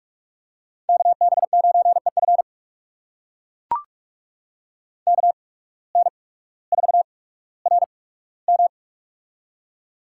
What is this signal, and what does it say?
Morse code, heard as a keyed tone, about 44 wpm, spaced as for 19 wpm, 700 Hz. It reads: KB9EF KN4RM
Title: USA Call Signs - 5 Characters - 8x Character Spacing - 45wpm